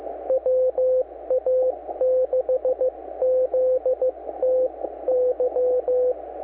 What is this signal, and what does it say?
Morse code, heard as a keyed tone, about 15 wpm, spaced as for 13 wpm, 510 Hz. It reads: WA6ZTY
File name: wa6zty-fall-09-40m-snip3